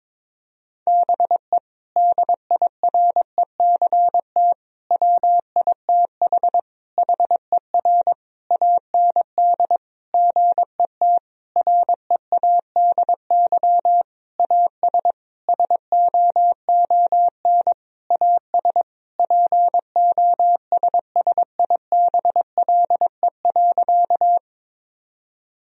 Morse code, heard as a keyed tone, about 22 wpm, 700 Hz. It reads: BE DIRECT WITH HER AND GET READY AS SOON AS POSSIBLE.
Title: Sentences from Top 1000 Words - Review Part 2 - 22wpm